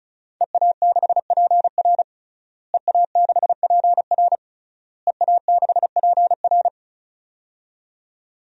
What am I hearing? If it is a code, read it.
Morse code, heard as a keyed tone, about 34 wpm, 700 Hz. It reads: EA6PR EA6PR EA6PR
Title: DXCC-100 Call Signs - 5 Characters - Repeated 3x - 35wpm